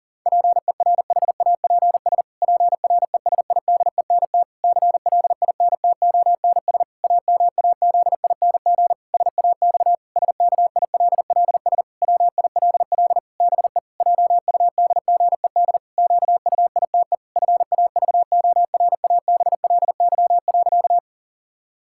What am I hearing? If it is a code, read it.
Morse code, heard as a keyed tone, about 40 wpm, 700 Hz. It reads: PERHAPS PRESIDENT CLINTONS AMAZING SAX SKILLS WILL BE JUDGED QUITE FAVORABLY.
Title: Pangram Challenge v2 - 40wpm